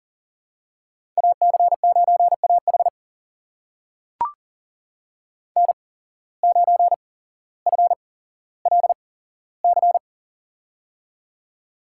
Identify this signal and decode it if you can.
Morse code, heard as a keyed tone, about 39 wpm, spaced as for 17 wpm, 700 Hz. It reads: AC9AH N9FLC
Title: USA Call Signs - 5 Characters - 8x Character Spacing - 40wpm